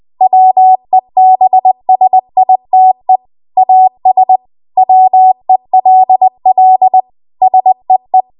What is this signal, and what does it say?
Morse code, heard as a keyed tone, about 20 wpm, 750 Hz. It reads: WEBSITE AS WELL SEE
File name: Archive / 20 WPM / 20WPM